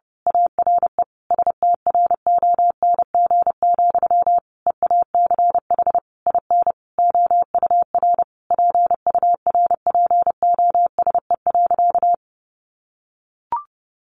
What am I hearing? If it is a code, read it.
Morse code, heard as a keyed tone, about 30 wpm, 700 Hz. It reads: ARE STRONG, EACH IN OUR PURPOSE.
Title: Sentences from Top 800 Words 30wpm